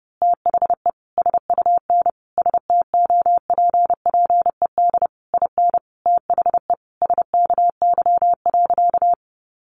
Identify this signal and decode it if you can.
Morse code, heard as a keyed tone, about 30 wpm, 700 Hz. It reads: THE SUN STOPPED IN THE SKY.
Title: A Sound Of Thunder 30wpm